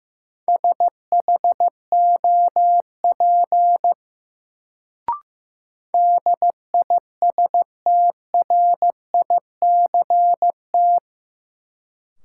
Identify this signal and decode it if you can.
Morse code, heard as a keyed tone, about 15 wpm, 700 Hz. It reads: SHOP DISTRICT
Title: Sentences from Top 1600 Words - New Word Drill - 15wpm